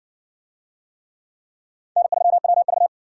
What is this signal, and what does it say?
Morse code, heard as a keyed tone, about 60 wpm, 700 Hz. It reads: N3WV